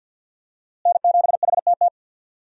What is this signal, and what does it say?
Morse code, heard as a keyed tone, about 47 wpm, 700 Hz. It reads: N7HTT
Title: USA Call Signs - 5 Characters - Repeated 2x - 50wpm